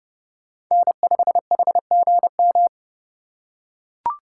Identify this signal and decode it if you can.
Morse code, heard as a keyed tone, about 31 wpm, 700 Hz. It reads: N5HGM
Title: USA Call Signs - 4x Character Spacing - 30wpm